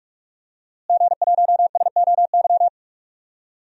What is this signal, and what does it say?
Morse code, heard as a keyed tone, about 46 wpm, 700 Hz. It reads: G1SOY
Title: DXCC-100 Call Signs - 5 Characters - Repeated 2x - 45wpm